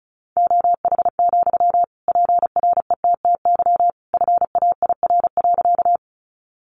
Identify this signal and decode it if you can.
Morse code, heard as a keyed tone, about 34 wpm, 700 Hz. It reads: OH, PRETTY FAIR.
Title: Sentences from Top 1300 Words 35wpm